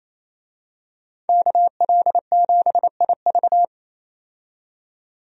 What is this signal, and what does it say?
Morse code, heard as a keyed tone, about 28 wpm, 700 Hz. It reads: KL7IV
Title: DXCC-100 Call Signs - 5 Characters - 28wpm